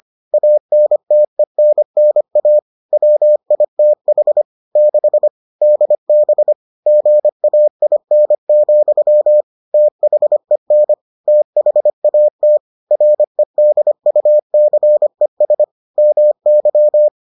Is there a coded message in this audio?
Yes: Morse code from 0.3 to 17.1 s, heard as a keyed tone, about 25 wpm, 600 Hz. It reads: ANTENNA WITH 6 DB GAIN, THEN THAT REDUCES MY